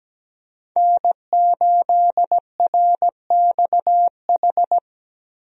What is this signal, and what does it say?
Morse code, heard as a keyed tone, about 17 wpm, 700 Hz. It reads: N8RXH
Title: USA Call Signs - 5 Characters - Repeated 3x - 17wpm